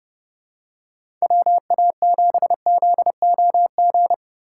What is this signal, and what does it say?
Morse code, heard as a keyed tone, about 30 wpm, 700 Hz. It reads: WA7ZOG
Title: USA Call Signs - 6 Characters - 3x Character Spacing - 30wpm